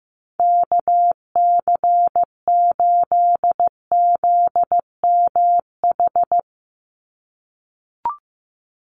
Morse code, heard as a keyed tone, about 15 wpm, 700 Hz. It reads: KC8ZMH
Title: USA Call Signs 15wpm